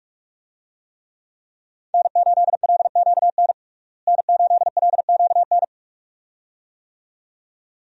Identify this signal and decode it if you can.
Morse code, heard as a keyed tone, about 47 wpm, 700 Hz. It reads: N8LQN N8LQN
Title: USA Call Signs - 5 Characters - Repeated 2x - 45wpm